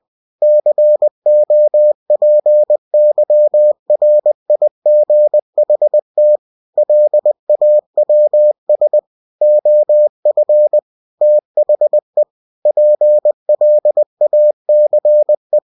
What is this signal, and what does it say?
Morse code, heard as a keyed tone, about 20 wpm, 600 Hz. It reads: COPYRIGHT LAWS OF THE PLACE
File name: cw-books / winnie-the-pooh / winnie-20wpm-page-0114